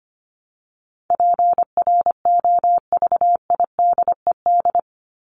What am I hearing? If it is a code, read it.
Morse code, heard as a keyed tone, about 25 wpm, 700 Hz. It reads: PROVIDED
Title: Top 1751-2000 Words 25wpm